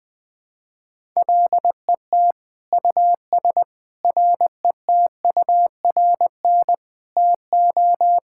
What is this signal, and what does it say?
Morse code, heard as a keyed tone, about 20 wpm, 700 Hz. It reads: LET US RETURN TO
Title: Sentences from Top 1000 Words - Review Part 6 - 20wpm